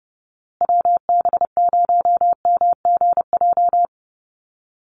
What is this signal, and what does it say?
Morse code, heard as a keyed tone, about 30 wpm, 700 Hz. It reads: WB0MGJ
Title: USA Call Signs Repeated 2x 30wpm